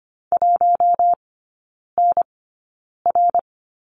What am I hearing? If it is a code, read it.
Morse code, heard as a keyed tone, about 25 wpm, spaced as for 9 wpm, 690 Hz. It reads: 1NR
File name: Lesson 15e - Number 5 - Three Characters 25wpm